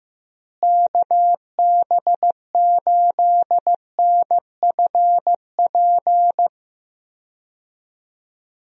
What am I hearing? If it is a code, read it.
Morse code, heard as a keyed tone, about 15 wpm, 700 Hz. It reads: KB8NFP
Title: USA Call Signs - 6 Characters - Repeated 2x - 15wpm